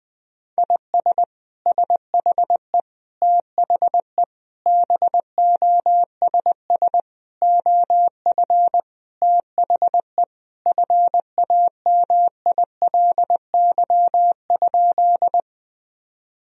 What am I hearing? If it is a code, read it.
Morse code, heard as a keyed tone, about 20 wpm, 700 Hz. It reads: IS SHE THE BOSS OF THE FAMILY?
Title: Sentences from Top 1000 Words - Review Part 4 - 20wpm